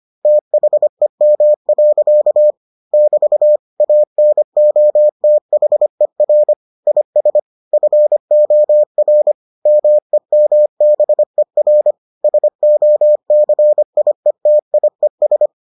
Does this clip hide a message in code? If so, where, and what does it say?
Yes: Morse code from 0.3 to 15.5 s, heard as a keyed tone, about 25 wpm, 600 Hz. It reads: THEM. <BT> ANOTHER IS FOR MEMBER SOCIETIES